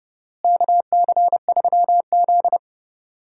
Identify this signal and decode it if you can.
Morse code, heard as a keyed tone, about 32 wpm, 700 Hz. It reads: KC3Z
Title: USA Call Signs - 4 Characters - Repeated 3x - 30wpm